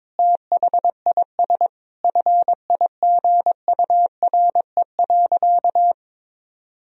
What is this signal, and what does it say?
Morse code, heard as a keyed tone, about 22 wpm, 700 Hz. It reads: THIS FIGURE.
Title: Sentences from Top 1000 Words - Review Part 2 - 22wpm